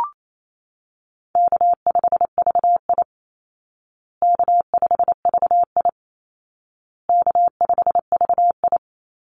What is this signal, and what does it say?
Morse code, heard as a keyed tone, about 28 wpm, 700 Hz. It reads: K5VI K5VI K5VI
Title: Easy Call Signs Repeated 3x - 28wpm